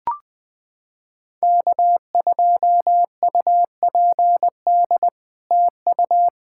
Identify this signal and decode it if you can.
Morse code, heard as a keyed tone, about 21 wpm, 700 Hz. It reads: K2UPD TU
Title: POTA QSO Practice - 20wpm